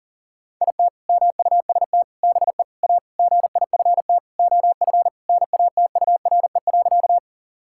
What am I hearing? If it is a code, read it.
Morse code, heard as a keyed tone, about 40 wpm, 700 Hz. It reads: IT MUST BE A GIFT OF NATURE.
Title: Sentences from Top 1000 Words - Review Part 6 - 40wpm